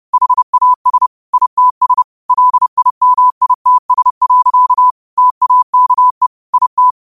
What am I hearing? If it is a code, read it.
Morse code, heard as a keyed tone, about 30 wpm, 1000 Hz. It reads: HAS ITS LIMITS. TAKE IT
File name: qotdfiles / 30wpm